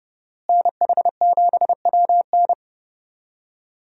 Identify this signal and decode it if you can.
Morse code, heard as a keyed tone, about 30 wpm, 700 Hz. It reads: NH7WN
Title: USA Call Signs - 5 Characters - Repeated 2x - 30wpm